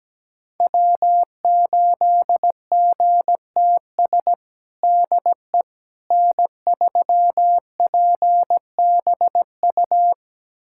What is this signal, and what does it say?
Morse code, heard as a keyed tone, about 17 wpm, 700 Hz. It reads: W8GTS DE N3PBU
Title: USA Callsign DE Callsign - 5 Characters - 17wpm